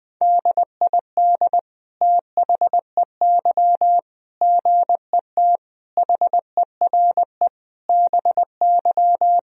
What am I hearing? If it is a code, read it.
Morse code, heard as a keyed tone, about 21 wpm, 700 Hz. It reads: DID THEY GET HERE BY
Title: Sentences from Top 1000 Words - Review Part 3 - 20wpm